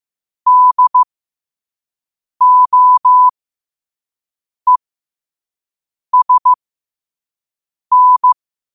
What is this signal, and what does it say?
Morse code, heard as a keyed tone, about 15 wpm, spaced as for 5 wpm, 1000 Hz. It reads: DOESN
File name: qotdfiles / 5wpm